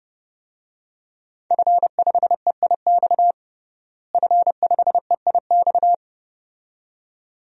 Morse code, heard as a keyed tone, about 30 wpm, 700 Hz. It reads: F5EIX F5EIX
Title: DXCC-100 Call Signs - 5 Characters - Repeated 2x - 30wpm